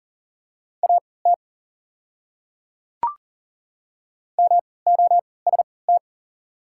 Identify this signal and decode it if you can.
Morse code, heard as a keyed tone, about 40 wpm, spaced as for 23 wpm, 700 Hz. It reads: AT MOST
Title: 3x Character Spacing - Top 200 Words - 40wpm